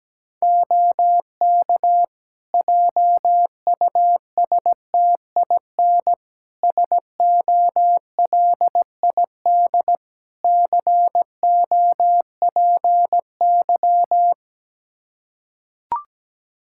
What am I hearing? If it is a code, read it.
Morse code, heard as a keyed tone, about 17 wpm, 700 Hz. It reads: OK JUSTIN SOLID COPY
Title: Practice QSO - Exchange 3 - More Info - 17wpm